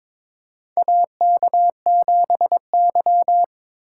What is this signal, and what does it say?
Morse code, heard as a keyed tone, about 22 wpm, 700 Hz. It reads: AK7Y
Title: USA Call Signs - 3x Character Spacing - 22wpm